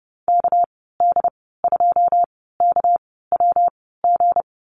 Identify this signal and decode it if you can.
Morse code, heard as a keyed tone, about 30 wpm, spaced as for 17 wpm, 700 Hz. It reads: KD2KWG
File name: Lesson 37g - Letter x - Call Signs 30wpm